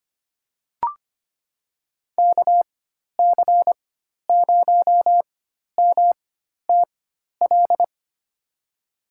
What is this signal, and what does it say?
Morse code, heard as a keyed tone, about 25 wpm, spaced as for 12 wpm, 700 Hz. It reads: KC0MTL
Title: USA Call Signs - 6 Characters - 4x Character Spacing - 25wpm